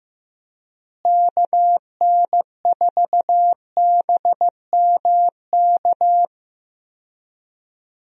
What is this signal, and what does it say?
Morse code, heard as a keyed tone, about 15 wpm, 700 Hz. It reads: KN4BMK
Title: USA Call Signs - 6 Characters - Repeated 3x - 15wpm